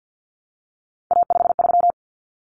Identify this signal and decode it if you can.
Morse code, heard as a keyed tone, about 48 wpm, 700 Hz. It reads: A53
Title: Common Cut Numbers - 3-Digit 50wpm